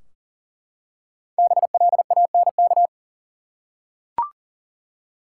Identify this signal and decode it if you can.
Morse code, heard as a keyed tone, about 40 wpm, 700 Hz. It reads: BLANK
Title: Five Letter Words - 40wpm